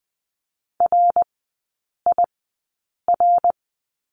Morse code, heard as a keyed tone, about 20 wpm, spaced as for 8 wpm, 700 Hz. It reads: RIR
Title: 074 Lesson 13e - Letter L - Three Characters 20wpm